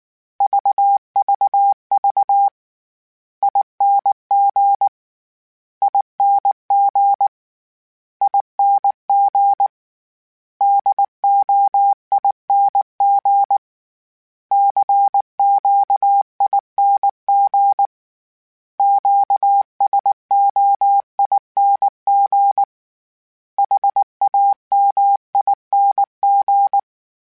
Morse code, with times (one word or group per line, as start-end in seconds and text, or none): 0.4-2.5 VVV
3.4-4.9 ING
5.8-7.3 ING
8.2-9.7 ING
10.6-13.6 DOING
14.5-17.9 CQING
18.8-22.6 QSOING
23.6-26.8 HAMING